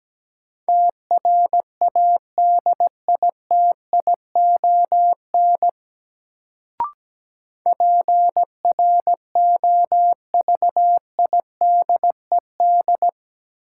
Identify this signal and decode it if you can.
Morse code, heard as a keyed tone, about 17 wpm, 700 Hz. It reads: TRADITION PROVIDED
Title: Sentences from Top 1500 Words - New Word Drill - 17wpm